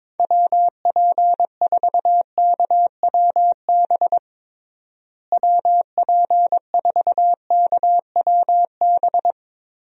Morse code, heard as a keyed tone, about 22 wpm, 700 Hz. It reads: WP4KWB WP4KWB
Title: USA Call Signs - 6 Characters - Repeated 3x - 22wpm